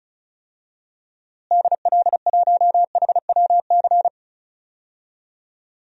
Morse code, heard as a keyed tone, about 34 wpm, 700 Hz. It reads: DL1HWC
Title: DXCC-100 Call Signs - 6 Characters - Repeated 3x - 35wpm